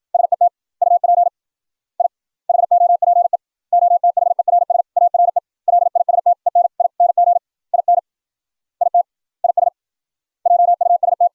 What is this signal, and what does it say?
Morse code, heard as a keyed tone, about 56 wpm, 700 Hz. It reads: SET UP I HOPE OTHERS ARE LISTENING IN IT IS JUST